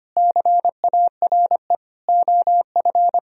Morse code, heard as a keyed tone, about 32 wpm, 700 Hz. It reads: CARE OF